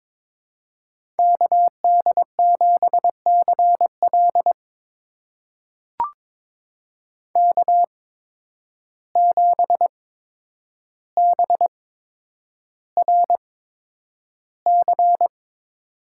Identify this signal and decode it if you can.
Morse code, heard as a keyed tone, about 22 wpm, spaced as for 9 wpm, 700 Hz. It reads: KD7CL K7BRC
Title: USA Call Signs - 5 Characters - 8x Character Spacing - 22wpm